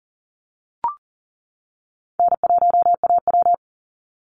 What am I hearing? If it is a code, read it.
Morse code, heard as a keyed tone, about 40 wpm, 700 Hz. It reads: N1AW
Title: NA SOTA Chaser Callsigns 40wpm